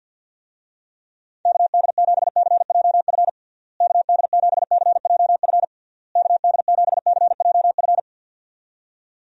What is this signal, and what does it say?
Morse code, heard as a keyed tone, about 52 wpm, 700 Hz. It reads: KD7CJF KD7CJF KD7CJF
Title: USA Call Signs - 6 Characters - Repeated 3x - 50wpm